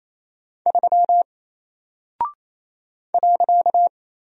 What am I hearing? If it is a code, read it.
Morse code, heard as a keyed tone, about 27 wpm, 700 Hz. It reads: 3 .
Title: Single Character (Letter:Number:Punctuation:Prosign) 28wpm